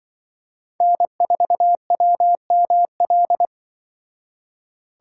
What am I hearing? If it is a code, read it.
Morse code, heard as a keyed tone, about 24 wpm, 700 Hz. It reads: N4WML
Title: Speed Racing - USA 1x3 Call Signs 20wpm